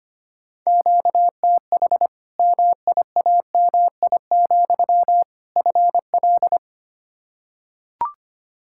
Morse code, heard as a keyed tone, about 25 wpm, 700 Hz. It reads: QTH MIAMI, FL
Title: Practice QSO - Exchange 1 - Beginning - 25wpm